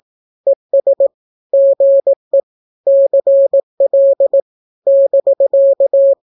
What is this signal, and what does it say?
Morse code, heard as a keyed tone, about 18 wpm, 550 Hz. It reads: ES GE CL <BK>
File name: naukapodstaw / skroty / skroty